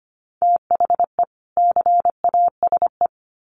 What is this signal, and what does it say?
Morse code, heard as a keyed tone, about 25 wpm, 700 Hz. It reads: THE CASE
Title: Sets of 2 Words 25wpm